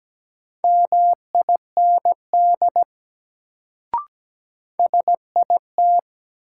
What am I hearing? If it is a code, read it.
Morse code, heard as a keyed tone, about 17 wpm, 700 Hz. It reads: MIND SIT
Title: Sentences from Top 500 Words - New Word Drill - 17wpm